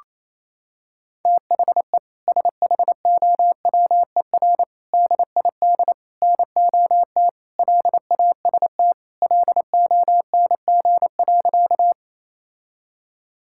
Morse code, heard as a keyed tone, about 27 wpm, 700 Hz. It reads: THE SHOWER DID NOT LAST LONG.
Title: Sentences from Top 1600 Words 28wpm